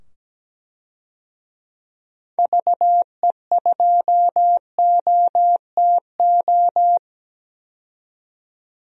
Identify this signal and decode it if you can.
Morse code, heard as a keyed tone, about 17 wpm, 700 Hz. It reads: VE2OTO
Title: DXCC-100 Call Signs - 6 Characters - 17wpm